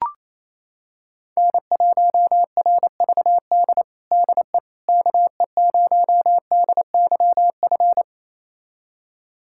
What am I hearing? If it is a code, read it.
Morse code, heard as a keyed tone, about 28 wpm, 700 Hz. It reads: N1RVD DE KE0DYF
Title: USA Callsign DE Callsign 28wpm